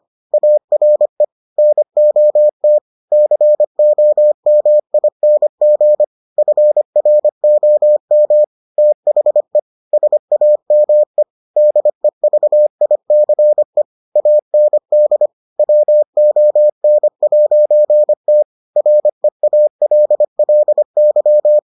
Morse code, heard as a keyed tone, about 25 wpm, 600 Hz. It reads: ARE NOT COMING FROM THE SAME DEVICE AND WON'T REALLY